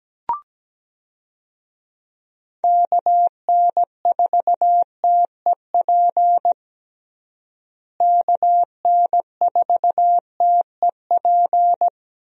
Morse code, heard as a keyed tone, about 17 wpm, 700 Hz. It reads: KN4TEP KN4TEP
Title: USA Call Signs - 6 Characters - Repeated 3x - 17wpm